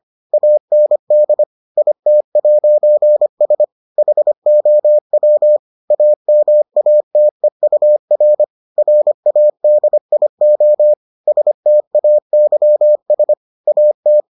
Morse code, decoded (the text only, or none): AND IT'S HOW AMATEUR RADIO STAYS AT